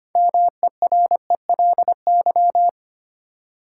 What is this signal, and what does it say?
Morse code, heard as a keyed tone, about 25 wpm, 700 Hz. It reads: MERELY
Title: Sentences from Top 1400 Words - New Word Drill Alt - 25wpm